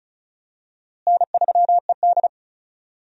Morse code, heard as a keyed tone, about 36 wpm, 700 Hz. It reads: N3ED
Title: USA Call Signs - 4 Characters - Repeated 3x - 35wpm